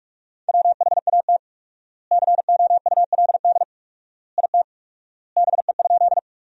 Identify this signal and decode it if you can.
Morse code, heard as a keyed tone, about 44 wpm, 700 Hz. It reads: WHAT COULD IT BE?
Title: 4x Word-Spacing - Sentences from Top 100 Words - 45wpm